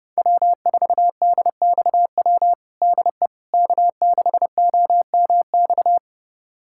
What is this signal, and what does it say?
Morse code, heard as a keyed tone, about 32 wpm, 700 Hz. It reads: W4DXW DE K6OMX
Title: USA Callsign DE Callsign - 5 Characters - 30wpm